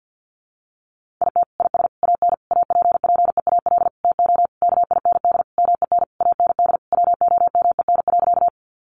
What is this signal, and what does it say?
Morse code, heard as a keyed tone, about 50 wpm, 700 Hz. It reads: IT IS AN APPEAL TO KIND MEN AND WOMEN.
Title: Sentences from Top 1300 Words 50wpm